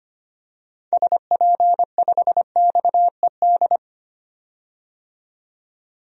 Morse code, heard as a keyed tone, about 25 wpm, 700 Hz. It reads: SP5XED
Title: DXCC-100 Call Signs - 6 Characters - 25wpm